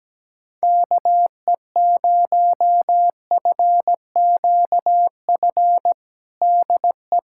Morse code, heard as a keyed tone, about 17 wpm, 700 Hz. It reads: KE0FQF DE
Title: USA Callsign DE Callsign - 6 Characters - 17wpm